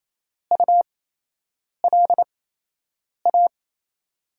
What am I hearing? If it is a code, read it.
Morse code, heard as a keyed tone, about 28 wpm, spaced as for 8 wpm, 700 Hz. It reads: ULA
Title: USA Call Signs - 8x Character Spacing - 28wpm